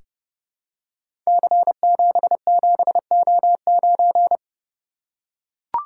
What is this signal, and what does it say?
Morse code, heard as a keyed tone, about 30 wpm, 700 Hz. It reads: C77O9
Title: Code Groups - 5 Letter:Numbers - 30wpm